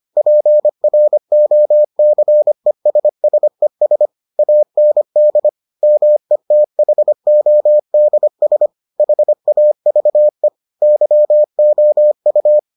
Morse code, heard as a keyed tone, about 25 wpm, 600 Hz. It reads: PROCESSES AND METHODS HAVE YOU